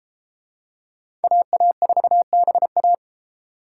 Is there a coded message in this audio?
Yes: Morse code from 1.2 to 2.9 s, heard as a keyed tone, about 29 wpm, 700 Hz. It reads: AA4BA